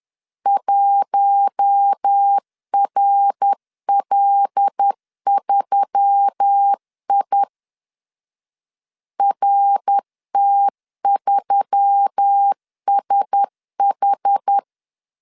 Morse code, heard as a keyed tone, about 10 wpm, 800 Hz. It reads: 1RL3I RT3SH